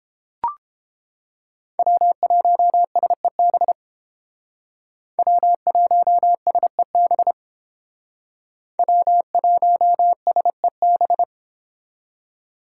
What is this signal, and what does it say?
Morse code, heard as a keyed tone, about 31 wpm, 700 Hz. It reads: W1SEB W1SEB W1SEB